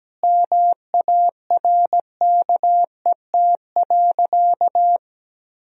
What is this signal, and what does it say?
Morse code, heard as a keyed tone, about 17 wpm, 700 Hz. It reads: MARKET.
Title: Sentences from Top 1000 Words - Review Part 3 - 17wpm